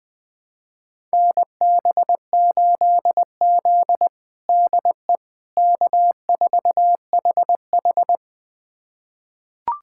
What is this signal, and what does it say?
Morse code, heard as a keyed tone, about 20 wpm, 700 Hz. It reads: NB8Z DE K4HH
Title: USA Callsign DE Callsign - 4 Characters - 20wpm